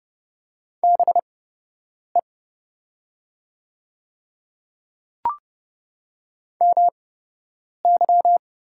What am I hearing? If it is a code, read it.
Morse code, heard as a keyed tone, about 29 wpm, spaced as for 8 wpm, 700 Hz. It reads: BE MY
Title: 8x Character Spacing - Top 100 Words - 30wpm